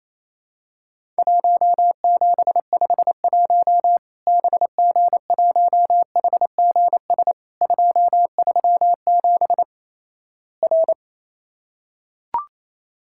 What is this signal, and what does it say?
Morse code, heard as a keyed tone, about 28 wpm, 700 Hz. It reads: 1751 BG1HGS 237 R
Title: Worked All Europe DX Contest - QTC Exchange - 28wpm